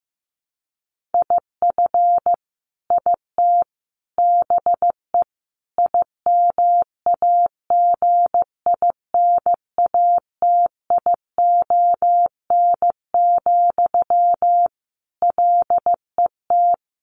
Morse code, heard as a keyed tone, about 15 wpm, 700 Hz. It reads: IF IT BE IMAGINATION, LET
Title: Sentences from Top 1300 Words 15wpm